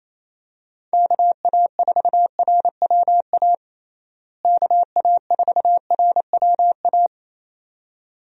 Morse code, heard as a keyed tone, about 28 wpm, 700 Hz. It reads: KA4RWA KA4RWA
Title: USA Call Signs - 6 Characters - Repeated 2x - 28wpm